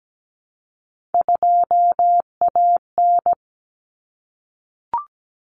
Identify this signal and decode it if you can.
Morse code, heard as a keyed tone, about 17 wpm, 700 Hz. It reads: 2AN
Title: Common Cut Numbers - 3-Digit 17wpm